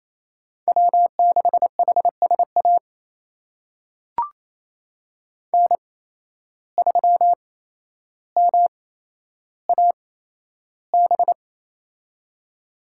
Morse code, heard as a keyed tone, about 28 wpm, spaced as for 12 wpm, 700 Hz. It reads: W6HSA N3MAB
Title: USA Call Signs - 5 Characters - 8x Character Spacing - 28wpm